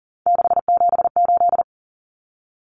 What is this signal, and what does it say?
Morse code, heard as a keyed tone, about 40 wpm, 700 Hz. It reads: 678
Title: Common Cut Numbers - 3-Digit 40wpm